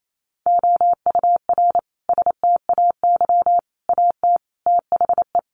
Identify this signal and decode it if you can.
Morse code, heard as a keyed tone, about 28 wpm, 700 Hz. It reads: OUR STAY AT THE
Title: Sentences from Top 1400 Words 28wpm